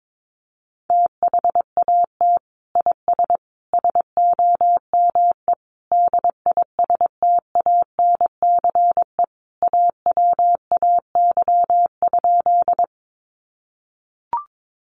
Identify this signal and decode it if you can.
Morse code, heard as a keyed tone, about 22 wpm, 700 Hz. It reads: THAT IS SOME DISTANCE AWAY?
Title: Sentences from Top 1200 Words 22wpm